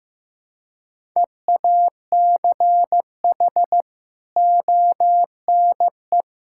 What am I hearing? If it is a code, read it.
Morse code, heard as a keyed tone, about 16 wpm, 700 Hz. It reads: EACH ONE
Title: Sentences from Top 1600 Words 15wpm